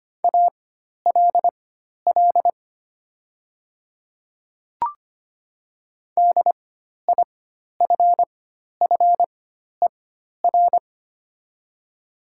Morse code, heard as a keyed tone, about 25 wpm, spaced as for 12 wpm, 700 Hz. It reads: ALL DIFFER
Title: 4x Character Spacing - Top 500 Words - 25wpm